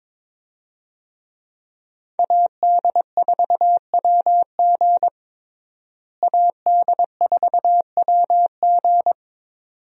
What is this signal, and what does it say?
Morse code, heard as a keyed tone, about 22 wpm, 700 Hz. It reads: AD4WG AD4WG
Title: USA Call Signs - 5 Characters - Repeated 2x - 22wpm